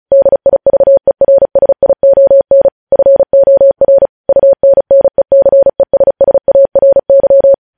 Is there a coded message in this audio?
Yes: Morse code from 0.1 to 7.5 s, heard as a keyed tone, about 36 wpm, 550 Hz. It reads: DIVERSION FOR UNNECESSARY